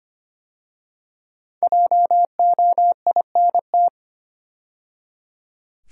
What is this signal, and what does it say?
Morse code, heard as a keyed tone, about 25 wpm, 700 Hz. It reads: JOINT
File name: 5 Letter Words - First 3 Letters Given - 25wpm